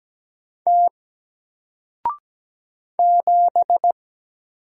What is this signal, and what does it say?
Morse code, heard as a keyed tone, about 17 wpm, 700 Hz. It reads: T 7
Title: Single Letter-Number Alt 17wpm